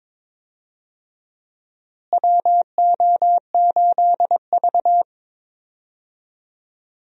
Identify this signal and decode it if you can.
Morse code, heard as a keyed tone, about 22 wpm, 700 Hz. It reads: WO8V
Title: USA Call Signs - 4 Characters - Repeated 2x - 22wpm